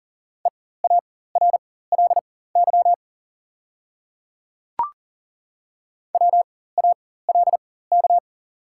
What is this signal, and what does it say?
Morse code, heard as a keyed tone, about 37 wpm, spaced as for 18 wpm, 700 Hz. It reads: EARLY WALK
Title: 4x Character Spacing - Top 500 Words - 40wpm